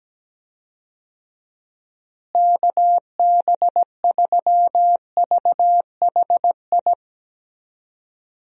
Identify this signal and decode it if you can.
Morse code, heard as a keyed tone, about 17 wpm, 700 Hz. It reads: KB3VHI
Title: USA Call Signs - 6 Characters - Repeated 2x - 17wpm